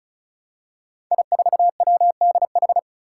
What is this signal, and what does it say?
Morse code, heard as a keyed tone, about 34 wpm, 700 Hz. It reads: I4WDH
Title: Five Challenging Letter-Number Combinations 35wpm